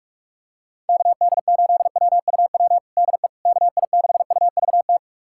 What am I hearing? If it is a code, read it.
Morse code, heard as a keyed tone, about 44 wpm, 700 Hz. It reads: KD8WUW DE KI6UVT
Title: USA Callsign DE Callsign - 6 Characters - 45wpm